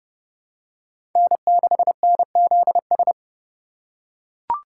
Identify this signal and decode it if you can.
Morse code, heard as a keyed tone, about 30 wpm, 700 Hz. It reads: N6NZS
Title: USA Call Signs - 5 Characters - 4x Character Spacing - 30wpm